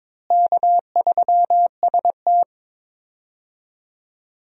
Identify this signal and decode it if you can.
Morse code, heard as a keyed tone, about 22 wpm, 700 Hz. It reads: K3ST
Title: USA Call Signs - 4 Characters - Repeated 2x - 22wpm